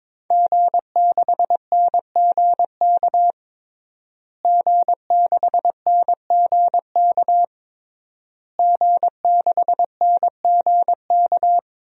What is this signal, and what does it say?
Morse code, heard as a keyed tone, about 23 wpm, 700 Hz. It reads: G6NGK G6NGK G6NGK
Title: DXCC-100 Call Signs - 5 Characters - Repeated 3x - 22wpm